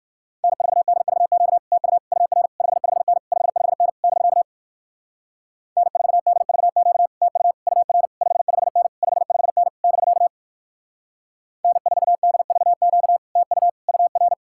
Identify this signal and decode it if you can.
Morse code, heard as a keyed tone, about 56 wpm, 700 Hz. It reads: N4DVQ TU UR 55N 55N <BK> N4DVQ TU UR 55N 55N <BK> N4DVQ TU UR
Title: Speed-Racing Common QSO Phrases Encore - 45wpm